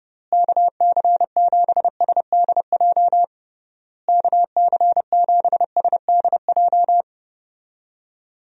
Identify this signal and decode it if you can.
Morse code, heard as a keyed tone, about 32 wpm, 700 Hz. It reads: KC7SDJ KC7SDJ
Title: USA Call Signs - 6 Characters - Repeated 3x - 30wpm